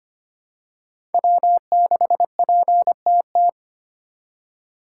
Morse code, heard as a keyed tone, about 25 wpm, 700 Hz. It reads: W6PTT
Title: USA Call Signs - 5 Characters - 4x Character Spacing - 25wpm